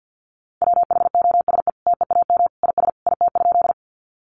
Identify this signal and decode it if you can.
Morse code, heard as a keyed tone, about 50 wpm, 700 Hz. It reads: WHOSE TEAM IS IT?
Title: Sentences from Top 1500 Words 50wpm